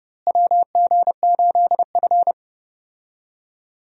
Morse code, heard as a keyed tone, about 30 wpm, 700 Hz. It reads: WG8F
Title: USA Call Signs - 4 Characters - Repeated 2x - 30wpm